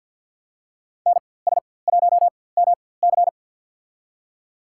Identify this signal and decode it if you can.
Morse code, heard as a keyed tone, about 47 wpm, spaced as for 23 wpm, 700 Hz. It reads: NS1MC
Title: USA Call Signs - 5 Characters - 4x Character Spacing - 50wpm